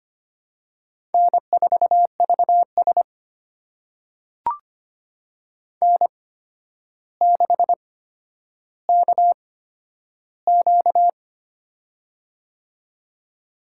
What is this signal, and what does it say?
Morse code, heard as a keyed tone, about 25 wpm, spaced as for 11 wpm, 700 Hz. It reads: N4VS N6KQ
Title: USA Call Signs - 4 Characters - 8x Character Spacing - 25wpm